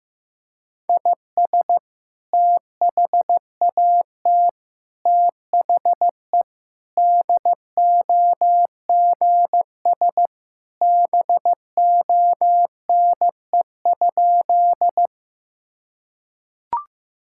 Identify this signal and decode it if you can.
Morse code, heard as a keyed tone, about 15 wpm, 700 Hz. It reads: IS THAT THE DOGS BONE?
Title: Sentences from Top 1600 Words 15wpm